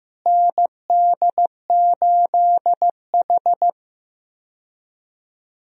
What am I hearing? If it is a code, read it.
Morse code, heard as a keyed tone, about 15 wpm, 700 Hz. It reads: ND8H
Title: USA Call Signs - 4 Characters - Repeated 3x - 15wpm